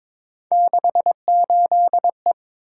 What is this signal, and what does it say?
Morse code, heard as a keyed tone, about 23 wpm, 700 Hz. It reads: 68E
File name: practice / 22wpm / Three Letter-Number Combinations - 22wpm